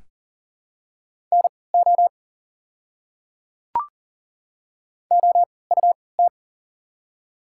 Morse code, heard as a keyed tone, about 40 wpm, spaced as for 23 wpm, 700 Hz. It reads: NO OUT
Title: 3x Character Spacing - Top 100 Words - 40wpm